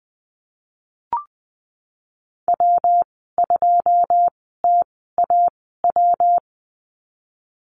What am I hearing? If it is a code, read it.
Morse code, heard as a keyed tone, about 20 wpm, spaced as for 15 wpm, 700 Hz. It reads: W2TAW
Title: 128 Lesson 21g - Number 6 - Call Signs 20wpm